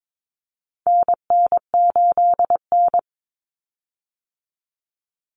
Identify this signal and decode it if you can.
Morse code, heard as a keyed tone, about 22 wpm, 700 Hz. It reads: NN8N
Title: Common Cut Numbers - 4-Digit 22wpm